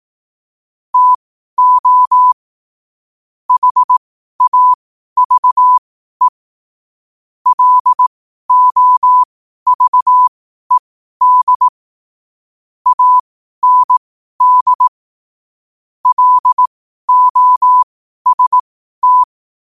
Morse code, heard as a keyed tone, about 18 wpm, spaced as for 12 wpm, 1000 Hz. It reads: TO HAVE LOVED AND LOST
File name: qotdfiles / 12wpm